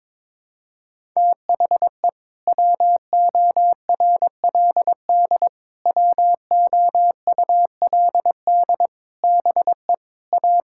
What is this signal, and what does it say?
Morse code, heard as a keyed tone, about 22 wpm, 700 Hz. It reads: THE WORLD WOULD BE A
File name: practice / 22wpm / Sets of 8 Words - 22wpm